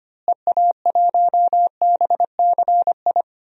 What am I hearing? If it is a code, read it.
Morse code, heard as a keyed tone, about 25 wpm, 700 Hz. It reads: EA1BCI
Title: DXCC-100 Call Signs - 6 Characters - Repeated 3x - 25wpm